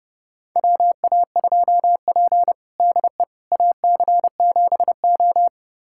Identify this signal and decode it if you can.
Morse code, heard as a keyed tone, about 32 wpm, 700 Hz. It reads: WA2P DE AC7O
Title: USA Callsign DE Callsign - 4 Characters - 30wpm